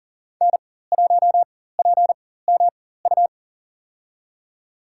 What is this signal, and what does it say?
Morse code, heard as a keyed tone, about 40 wpm, spaced as for 19 wpm, 700 Hz. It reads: N1PMU
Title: USA Call Signs - 5 Characters - 4x Character Spacing - 40wpm